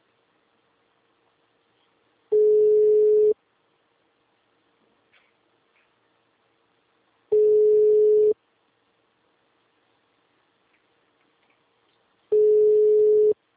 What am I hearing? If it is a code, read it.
Ringback tone (Europe, 425 Hz).
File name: out- -2018- - - .2